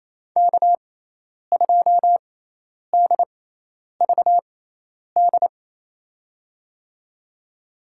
Morse code, heard as a keyed tone, about 28 wpm, spaced as for 10 wpm, 700 Hz. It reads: K2DVD
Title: USA Call Signs - 5 Characters - 6x Character Spacing - 28wpm